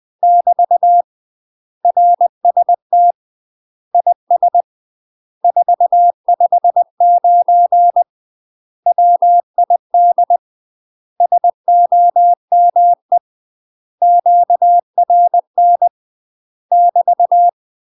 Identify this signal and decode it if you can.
Morse code, heard as a keyed tone, about 20 wpm, 700 Hz. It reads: <BT> RST IS 459 WID SOME QRN <BT>
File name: UK MASTER QSOs TRAINING / 20WPM / UK MASTER QSOs 0064